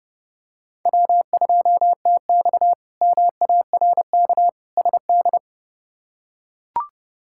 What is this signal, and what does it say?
Morse code, heard as a keyed tone, about 30 wpm, 700 Hz. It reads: W2TX MARK SD
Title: K1USN SST QSO Practice - 30wpm